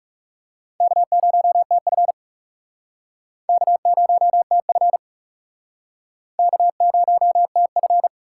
Morse code, heard as a keyed tone, about 40 wpm, 700 Hz. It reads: K0TF K0TF K0TF